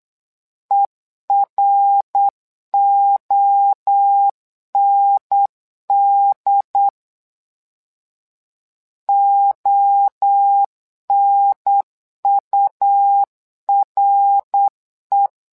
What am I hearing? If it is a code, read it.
Morse code, heard as a keyed tone, about 8 wpm, 800 Hz. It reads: EROND ONURE
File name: LEZIONI / 3 MWRO / 3 MWRO 8wpm 2606